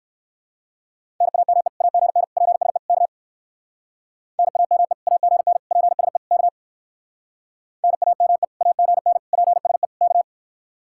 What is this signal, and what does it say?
Morse code, heard as a keyed tone, about 57 wpm, 700 Hz. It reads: NAME AGN PSE K NAME AGN PSE K NAME AGN PSE K